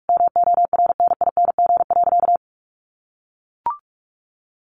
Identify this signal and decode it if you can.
Morse code, heard as a keyed tone, about 44 wpm, 700 Hz. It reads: MORNING.
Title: Sentences from Top 1500 Words 45wpm